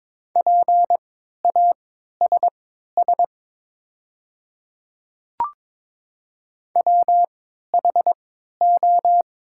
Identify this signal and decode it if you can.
Morse code, heard as a keyed tone, about 22 wpm, spaced as for 12 wpm, 700 Hz. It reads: PASS WHO
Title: 3x Character Spacing - Top 500 Words - 22wpm